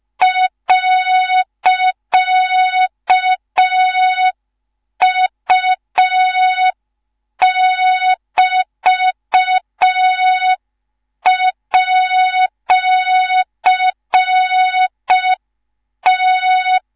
Morse code, heard as a keyed tone, about 5 wpm, 3700 Hz. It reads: .U<BT>@T
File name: Morze-HA2MN / hang / 19 GYAKORLAT